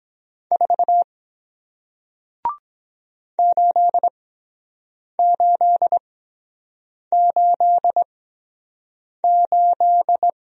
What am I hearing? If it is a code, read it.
Morse code, heard as a keyed tone, about 23 wpm, spaced as for 7 wpm, 700 Hz. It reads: 4 8888